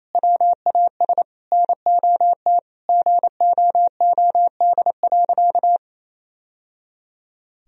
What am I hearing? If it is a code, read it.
Morse code, heard as a keyed tone, about 28 wpm, 700 Hz. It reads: WAS NOT GOOD.
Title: Sentences from Top 1000 Words - Review Part 3 - 28wpm